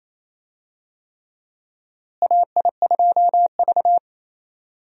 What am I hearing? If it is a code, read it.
Morse code, heard as a keyed tone, about 28 wpm, 700 Hz. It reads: AI2V
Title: USA Call Signs - 4 Characters - Repeated 3x - 28wpm